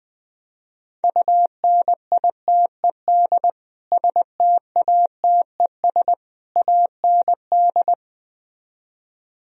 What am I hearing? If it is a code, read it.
Morse code, heard as a keyed tone, about 20 wpm, 700 Hz. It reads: UNITED STATES AND
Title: Speed-Racing - Sets of 3 Words 17wpm